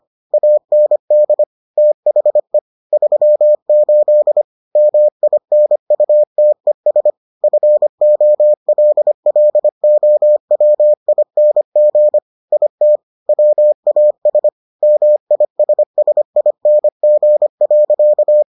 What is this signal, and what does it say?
Morse code, heard as a keyed tone, about 25 wpm, 600 Hz. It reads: AND THE 38 MINUTES FOLLOWING IT WAS MISSING.